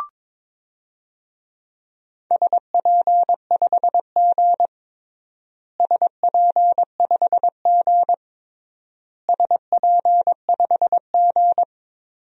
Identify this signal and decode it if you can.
Morse code, heard as a keyed tone, about 22 wpm, 700 Hz. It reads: SP5G SP5G SP5G
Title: DXCC-100 Call Signs - 4 Characters - Repeated 3x - 22wpm